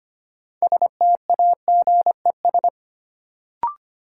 Morse code, heard as a keyed tone, about 25 wpm, 700 Hz. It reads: STAGES
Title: Sets of 5 Words Encore - New Word Drill - 25wpm